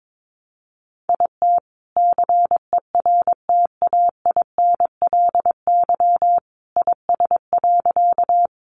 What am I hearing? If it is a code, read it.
Morse code, heard as a keyed tone, about 22 wpm, 700 Hz. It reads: IT CERTAINLY IS.
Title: Sentences from Top 900 Words 22wpm